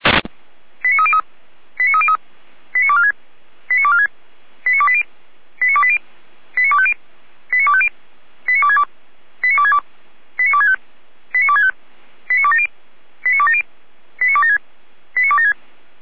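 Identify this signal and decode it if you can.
Five-tone selective call (ZVEI), heard as quick runs of five tones, tones of about 70 ms. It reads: 89303 89303 89246 89246 89280 89280 89250 89250 89262 89262 89265 89265 89270 89270 89276 89276
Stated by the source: Received on an SDR.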